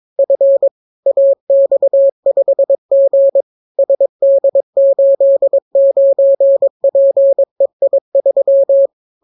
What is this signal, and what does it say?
Morse code, heard as a keyed tone, about 22 wpm, 550 Hz. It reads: F AX5G SD89PEI3